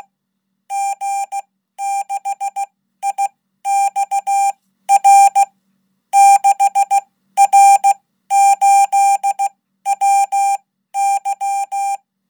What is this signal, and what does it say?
Morse code, heard as a keyed tone, about 16 wpm, spaced as for 12 wpm, 800 Hz. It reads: G6IXR 6R8WY
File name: Code